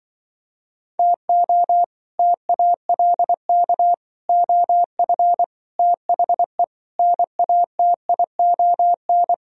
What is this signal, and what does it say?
Morse code, heard as a keyed tone, about 24 wpm, 700 Hz. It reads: TO TALK OF THE NATION
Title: Speed-Racing - Sets of 5 Words 20wpm